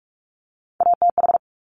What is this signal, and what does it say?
Morse code, heard as a keyed tone, about 46 wpm, 700 Hz. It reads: ATH
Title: Top Three Letter Combinations in Words 45wpm